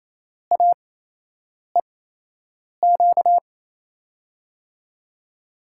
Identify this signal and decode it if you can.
Morse code, heard as a keyed tone, about 28 wpm, spaced as for 8 wpm, 700 Hz. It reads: AEQ
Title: USA Call Signs - 5 Characters - 8x Character Spacing - 28wpm